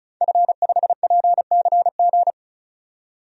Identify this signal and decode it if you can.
Morse code, heard as a keyed tone, about 34 wpm, 700 Hz. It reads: F5PCG
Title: DXCC-100 Call Signs - 5 Characters - 35wpm